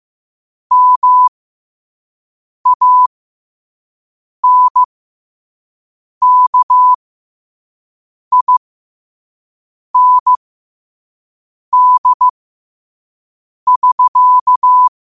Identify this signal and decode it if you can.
Morse code, heard as a keyed tone, about 15 wpm, spaced as for 5 wpm, 1000 Hz. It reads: MANKIND<SK>
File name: qotdfiles / 5wpm